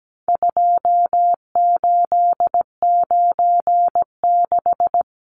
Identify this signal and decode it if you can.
Morse code, heard as a keyed tone, about 17 wpm, 700 Hz. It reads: 2896
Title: Numbers - 4-Digit 17wpm